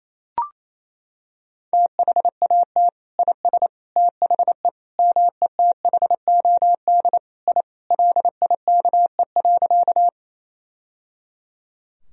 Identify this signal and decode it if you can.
Morse code, heard as a keyed tone, about 28 wpm, 700 Hz. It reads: THAT IS THE METHOD I LIKE.
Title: Sentences from Top 1000 Words - Review Part 3 - 28wpm